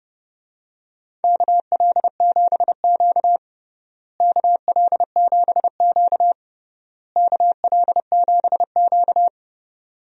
Morse code, heard as a keyed tone, about 32 wpm, 700 Hz. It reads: KL7Q KL7Q KL7Q
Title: DXCC-100 Call Signs - 4 Characters - Repeated 3x - 30wpm